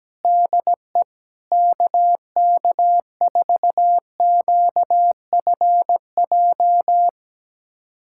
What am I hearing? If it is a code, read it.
Morse code, heard as a keyed tone, about 17 wpm, 700 Hz. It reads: DE KK4QFJ
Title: USA Callsign DE Callsign - 6 Characters - 17wpm